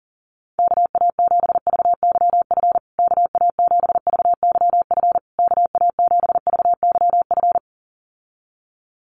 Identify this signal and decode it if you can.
Morse code, heard as a keyed tone, about 40 wpm, 700 Hz. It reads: KA7VYF KA7VYF KA7VYF
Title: USA Call Signs Repeated 3x 40wpm